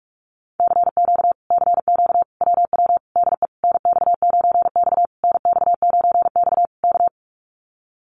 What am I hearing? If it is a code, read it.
Morse code, heard as a keyed tone, about 44 wpm, 700 Hz. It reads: CQ CQ WW DE NX9X NX9X K
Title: Common QSO Phrases Encore 45wpm